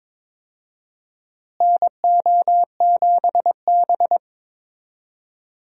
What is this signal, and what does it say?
Morse code, heard as a keyed tone, about 22 wpm, 700 Hz. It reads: NO7B
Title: USA Call Signs - 4 Characters - Repeated 2x - 22wpm